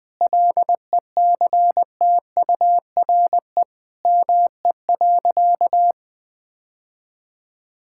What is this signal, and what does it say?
Morse code, heard as a keyed tone, about 20 wpm, 700 Hz. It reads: LECTURE ME.
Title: Sentences from Top 1600 Words 20wpm